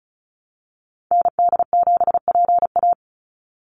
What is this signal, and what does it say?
Morse code, heard as a keyed tone, about 36 wpm, 700 Hz. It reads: ND7PA
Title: NA SOTA Activator Callsigns 35wpm